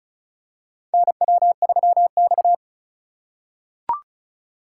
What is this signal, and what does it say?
Morse code, heard as a keyed tone, about 36 wpm, 700 Hz. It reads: NW3X
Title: USA Call Signs - 4 Characters - 3x Character Spacing - 35wpm